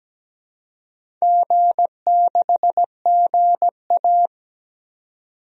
Morse code, heard as a keyed tone, about 17 wpm, 700 Hz. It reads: G6GA
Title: DXCC-100 Call Signs - 4 Characters - 17wpm